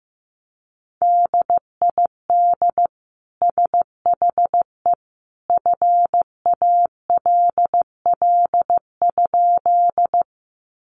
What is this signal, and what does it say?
Morse code, heard as a keyed tone, about 15 wpm, 700 Hz. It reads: DID SHE FALL?
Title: Sentences from Top 500 Words 15wpm